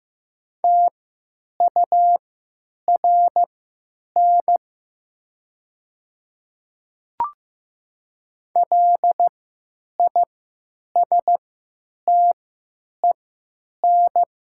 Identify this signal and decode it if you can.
Morse code, heard as a keyed tone, about 15 wpm, spaced as for 9 wpm, 700 Hz. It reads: TURN LISTEN
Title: 3x Character Spacing - Top 500 Words - 15wpm